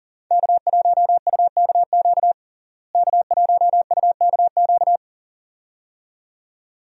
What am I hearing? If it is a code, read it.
Morse code, heard as a keyed tone, about 40 wpm, 700 Hz. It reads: K1UKQ K1UKQ
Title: USA Call Signs - 5 Characters - Repeated 2x - 40wpm